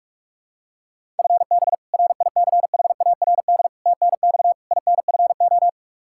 Morse code, heard as a keyed tone, about 41 wpm, 700 Hz. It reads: FB RICHARD TNX INFO
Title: Practice QSO - Exchange 3 - More Info - 45wpm